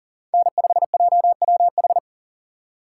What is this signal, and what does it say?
Morse code, heard as a keyed tone, about 40 wpm, 700 Hz. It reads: N5JWH
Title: USA Call Signs - 5 Characters - 2x Character Spacing - 40wpm